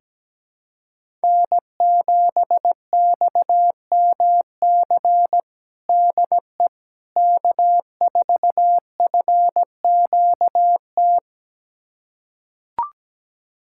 Morse code, heard as a keyed tone, about 17 wpm, 700 Hz. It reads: N7XMC DE K4FQT
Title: USA Callsign DE Callsign 17wpm